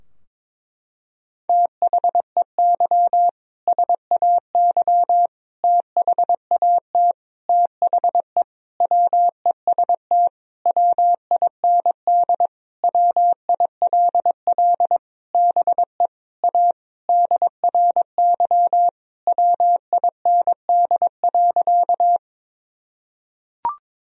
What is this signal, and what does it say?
Morse code, heard as a keyed tone, about 22 wpm, 700 Hz. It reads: THEY SAY THAT THE WEST WIND WILL BE A DRY WIND.
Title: Sentences from Top 1000 Words - Review Part 3 - 22wpm